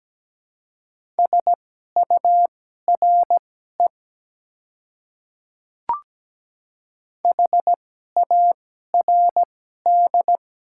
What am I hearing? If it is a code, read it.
Morse code, heard as a keyed tone, about 17 wpm, spaced as for 12 wpm, 700 Hz. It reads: SURE HARD
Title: 2x Character Spacing - Top 300 Words - 17wpm